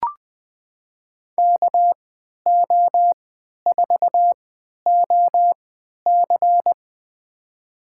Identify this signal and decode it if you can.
Morse code, heard as a keyed tone, about 20 wpm, spaced as for 11 wpm, 700 Hz. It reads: KO4OC
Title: USA Call Signs - 5 Characters - 3x Character Spacing - 20wpm